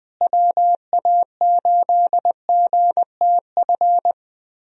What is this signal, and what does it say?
Morse code, heard as a keyed tone, about 20 wpm, 700 Hz. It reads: WA8GTF
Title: USA Call Signs - 6 Characters - Repeated 3x - 20wpm